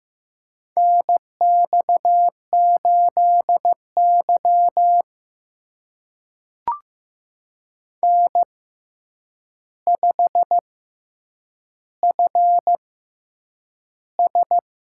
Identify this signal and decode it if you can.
Morse code, heard as a keyed tone, about 15 wpm, spaced as for 8 wpm, 700 Hz. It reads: NX8Y N5FS
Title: USA Call Signs - 4 Characters - 6x Character Spacing - 15wpm